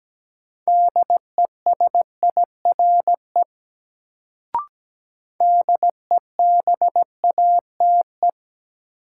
Morse code, heard as a keyed tone, about 17 wpm, 700 Hz. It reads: DESIRE DEBATE
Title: Sentences from Top 1100 Words - New Word Drill - 17wpm